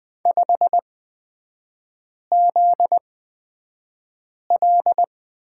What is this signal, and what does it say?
Morse code, heard as a keyed tone, about 20 wpm, spaced as for 5 wpm, 700 Hz. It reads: 5ZL
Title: 243 Lesson 39e - Letter z - Three Characters 20wpm